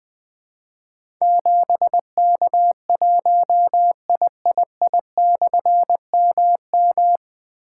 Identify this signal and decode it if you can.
Morse code, heard as a keyed tone, about 21 wpm, 700 Hz. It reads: 7K1III/MM
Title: Difficult Call Signs 20wpm